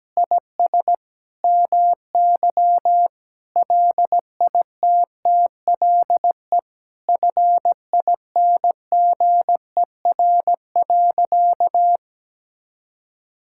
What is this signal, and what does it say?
Morse code, heard as a keyed tone, about 17 wpm, 700 Hz. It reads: IS MY LITTLE FINGER.
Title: Sentences from Top 1600 Words 17wpm